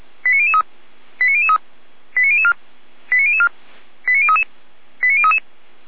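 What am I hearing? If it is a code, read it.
Five-tone selective call (ZVEI), heard as quick runs of five tones, tones of about 70 ms. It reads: 89003 89003 89004 89004 89033 89033
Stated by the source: Five-tone ZVEI selective call received on an SDR, from audio.f85360.feuerwehr-orlamuende.org fire brigade server.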